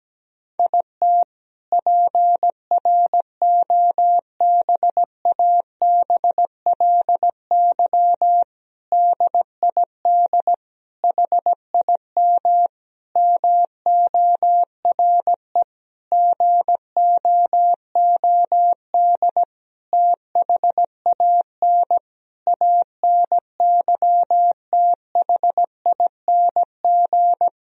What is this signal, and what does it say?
Morse code, heard as a keyed tone, about 17 wpm, 700 Hz. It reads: IT PROBABLY DID HIM MORE GOOD THAN ANYTHING
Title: Sentences from Top 1000 Words - Review Part 4 - 17wpm